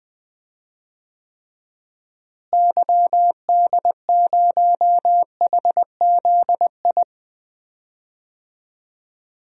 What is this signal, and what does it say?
Morse code, heard as a keyed tone, about 20 wpm, 700 Hz. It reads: YD0HZI
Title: DXCC-100 Call Signs - 6 Characters - Repeated 3x - 20wpm